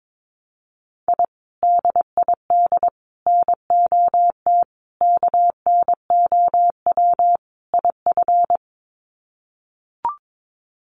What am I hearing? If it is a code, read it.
Morse code, heard as a keyed tone, about 22 wpm, 700 Hz. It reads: I DID NOT KNOW IF
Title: Sets of 5 Words Encore 22wpm